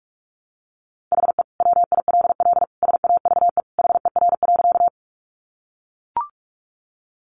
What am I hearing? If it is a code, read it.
Morse code, heard as a keyed tone, about 44 wpm, 700 Hz. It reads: HE WILL SAVE HER.
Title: Sentences from Top 1100 Words 45wpm